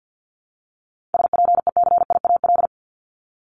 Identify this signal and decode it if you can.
Morse code, heard as a keyed tone, about 50 wpm, 700 Hz. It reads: SPECIAL
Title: Multi-letter Phonemes 50wpm